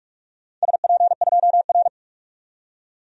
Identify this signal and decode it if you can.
Morse code, heard as a keyed tone, about 41 wpm, 700 Hz. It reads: SP2R
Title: DXCC-100 Call Signs - 4 Characters - 45wpm